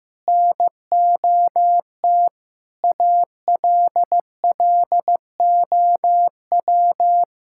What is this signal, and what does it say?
Morse code, heard as a keyed tone, about 16 wpm, 700 Hz. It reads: NOT ALLOW
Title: Sentences from Top 1000 Words - Review Part 5 - 15wpm